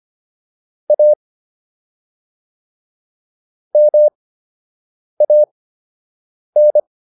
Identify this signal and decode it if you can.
Morse code, heard as a keyed tone, about 25 wpm, spaced as for 7 wpm, 600 Hz. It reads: A MAN